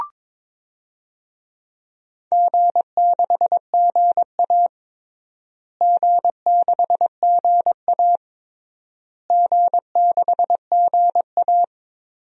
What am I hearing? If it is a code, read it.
Morse code, heard as a keyed tone, about 22 wpm, 700 Hz. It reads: G6GA G6GA G6GA
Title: DXCC-100 Call Signs - 4 Characters - Repeated 3x - 22wpm